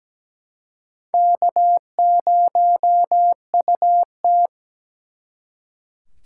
Morse code, heard as a keyed tone, about 20 wpm, 700 Hz. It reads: K0UT
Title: Speed Racing - USA 1x2 Call Signs 17wpm